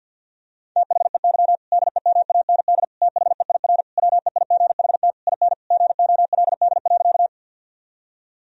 Morse code, heard as a keyed tone, about 50 wpm, 700 Hz. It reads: THEY DEMAND THEIR WEIGHT IN GOLD.
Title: Sentences from Top 1000 Words - Review Part 3 - 50wpm